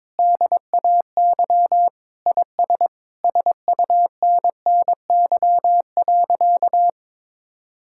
Morse code, heard as a keyed tone, about 22 wpm, 700 Hz. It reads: DAY IS SUNNY.
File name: practice / 22wpm / Sentences with 2-Word Phrases - 22wpm